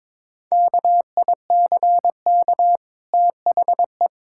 Morse code, heard as a keyed tone, about 22 wpm, 690 Hz. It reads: KICK THE
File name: practice / 22wpm / Easy Sentences with Hints - 22wpm